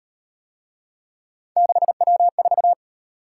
Morse code, heard as a keyed tone, about 37 wpm, 700 Hz. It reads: 6W4